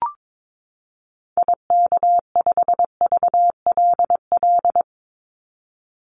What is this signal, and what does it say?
Morse code, heard as a keyed tone, about 22 wpm, 700 Hz. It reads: IK5VLL
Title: Call Signs - 22wpm